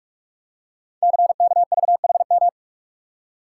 Morse code, heard as a keyed tone, about 44 wpm, 700 Hz. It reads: CKVHM
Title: Code Groups - 5 Letters - 45wpm